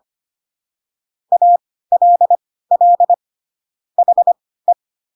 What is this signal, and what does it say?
Morse code, heard as a keyed tone, about 25 wpm, spaced as for 16 wpm, 700 Hz. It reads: ALL HE
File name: Pinocchio / Pinocchio ch3 10 25 ik1hge